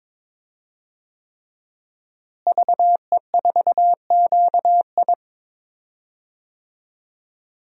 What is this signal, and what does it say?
Morse code, heard as a keyed tone, about 22 wpm, 700 Hz. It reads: VE4QI
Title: DXCC-100 Call Signs - 5 Characters - Repeated 3x - 22wpm